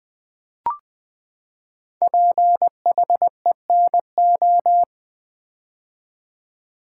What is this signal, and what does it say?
Morse code, heard as a keyed tone, about 20 wpm, 700 Hz. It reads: PHENO
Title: Build A Word - 5 Letters At A Time - 10-Letter Words - 20wpm